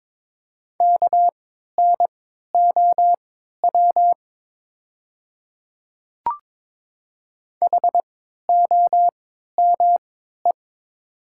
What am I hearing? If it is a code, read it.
Morse code, heard as a keyed tone, about 23 wpm, spaced as for 13 wpm, 700 Hz. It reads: KNOW HOME
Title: 3x Character Spacing - Top 200 Words - 22wpm